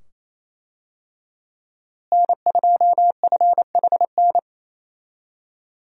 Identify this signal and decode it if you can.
Morse code, heard as a keyed tone, about 28 wpm, 700 Hz. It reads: N2FHN
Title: USA Call Signs - 5 Characters - Repeated 2x - 28wpm